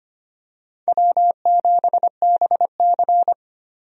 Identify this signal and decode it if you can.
Morse code, heard as a keyed tone, about 25 wpm, 700 Hz. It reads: W7BC
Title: USA Call Signs - 4 Characters - 25wpm